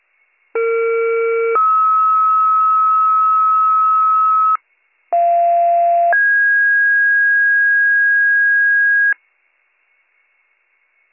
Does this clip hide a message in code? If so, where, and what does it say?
Yes: two-tone page from 0.5 to 9.1 s, heard as a one-second tone, then a three-second tone, about 460 then 1290 Hz, about 690 then 1650 Hz.